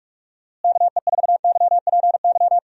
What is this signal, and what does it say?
Morse code, heard as a keyed tone, about 47 wpm, 700 Hz. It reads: KE4YPY
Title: USA Call Signs - 6 Characters - Repeated 3x - 45wpm